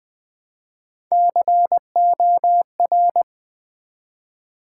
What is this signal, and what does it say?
Morse code, heard as a keyed tone, about 20 wpm, 690 Hz. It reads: COR
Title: Build A Word - 1 Letter At A Time - 4-Letter Words - 20wpm